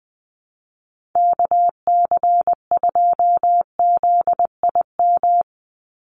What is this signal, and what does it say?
Morse code, heard as a keyed tone, about 20 wpm, 700 Hz. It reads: KC2ZIM
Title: USA Call Signs 20wpm